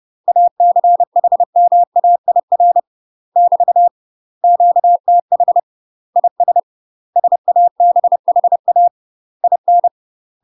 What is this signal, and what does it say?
Morse code, heard as a keyed tone, about 30 wpm, 700 Hz. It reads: ACHMAIR <BT> QTH IS SABHA IN
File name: UK MASTER QSOs TRAINING / 30WPM / UK MASTER QSOs 0049